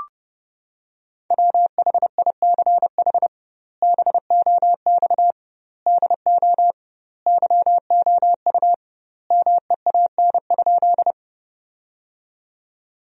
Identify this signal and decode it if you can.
Morse code, heard as a keyed tone, about 32 wpm, 700 Hz. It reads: WHICH BOX DO YOU MEAN?
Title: 2x Word-Spacing - Sentences from Top 500 Words - 30wpm